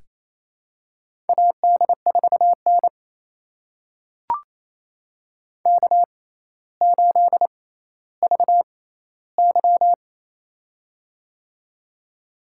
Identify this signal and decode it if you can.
Morse code, heard as a keyed tone, about 28 wpm, spaced as for 14 wpm, 700 Hz. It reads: AD4N K8VY
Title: USA Call Signs - 4 Characters - 6x Character Spacing - 28wpm